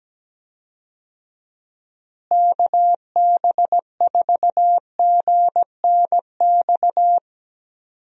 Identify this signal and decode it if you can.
Morse code, heard as a keyed tone, about 17 wpm, 700 Hz. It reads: KB4GNX
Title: USA Call Signs - 6 Characters - Repeated 2x - 17wpm